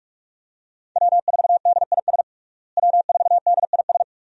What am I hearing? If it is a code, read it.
Morse code, heard as a keyed tone, about 42 wpm, 700 Hz. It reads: W4DIS W4DIS
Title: USA Call Signs - 5 Characters - Repeated 3x - 45wpm